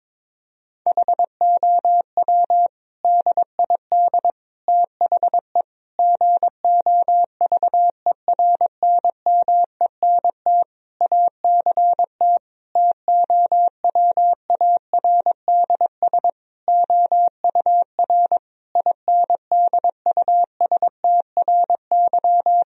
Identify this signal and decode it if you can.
Morse code, heard as a keyed tone, about 22 wpm, 700 Hz. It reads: HOW DID THE GOVERNMENT ACT TOWARDS OUR INDUSTRY
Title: Sentences from Top 1000 Words - Review Part 4 - 22wpm